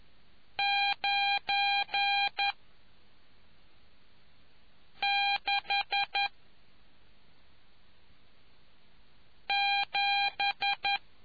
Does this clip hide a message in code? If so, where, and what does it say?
Yes: Morse code from 0.6 to 11.0 s, heard as a keyed tone, about 11 wpm, 800 Hz. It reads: 9 6 7